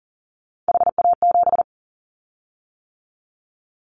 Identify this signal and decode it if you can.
Morse code, heard as a keyed tone, about 40 wpm, 700 Hz. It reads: HA7
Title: Callsign Prefixes for Top 50 DXCC - 40wpm